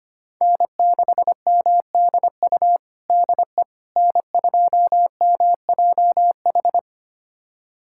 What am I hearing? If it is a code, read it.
Morse code, heard as a keyed tone, about 25 wpm, 700 Hz. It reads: N6MDU DE N2MJH
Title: USA Callsign DE Callsign - 5 Characters - 25wpm